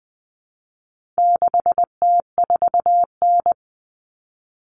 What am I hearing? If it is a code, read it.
Morse code, heard as a keyed tone, about 20 wpm, 700 Hz. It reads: 6T4N
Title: Common Cut Numbers - 4-Digit 20wpm